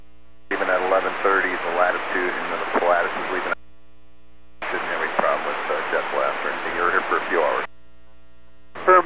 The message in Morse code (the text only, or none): none